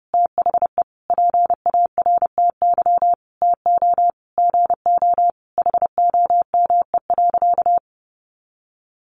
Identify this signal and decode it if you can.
Morse code, heard as a keyed tone, about 30 wpm, 700 Hz. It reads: THE PARTY TO GO HOME.
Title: Sentences from Top 1400 Words 30wpm